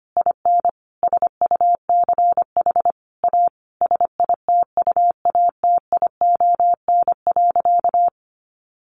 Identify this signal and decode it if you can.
Morse code, heard as a keyed tone, about 25 wpm, 700 Hz. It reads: IN SUCH A SITUATION.
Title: Sentences from Top 1200 Words 25wpm